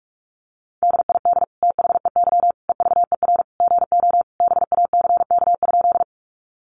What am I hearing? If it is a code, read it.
Morse code, heard as a keyed tone, about 44 wpm, 700 Hz. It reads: DID THEY EVER GO BACK?
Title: Sentences from Top 400 Words 45wpm